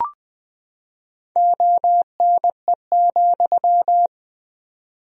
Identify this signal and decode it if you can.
Morse code, heard as a keyed tone, about 20 wpm, 700 Hz. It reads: ONE,
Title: 4x Word-Spacing - Sentences from Top 200 Words - 20wpm